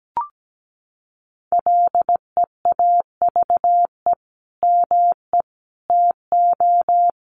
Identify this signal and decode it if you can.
Morse code, heard as a keyed tone, about 17 wpm, 700 Hz. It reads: LEAVE ME TO
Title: Sentences from Top 1400 Words 17wpm